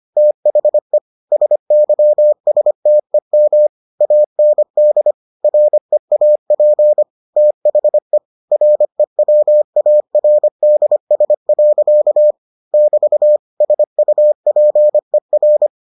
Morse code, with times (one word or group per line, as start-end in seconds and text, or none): 0.2-1.0 THE
1.3-3.7 SYSTEM
4.0-5.1 AND
5.4-7.0 REAP
7.4-8.2 THE
8.5-12.3 REWARDS.
12.7-13.4 <BT>
13.6-15.7 SUPER